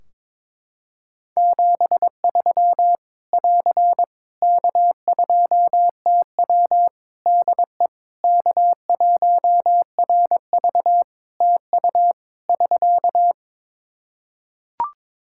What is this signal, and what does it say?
Morse code, heard as a keyed tone, about 22 wpm, 700 Hz. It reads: 73 <AR> K2TW DE K1RV TU <SK>
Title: K1USN SST QSO Practice 22wpm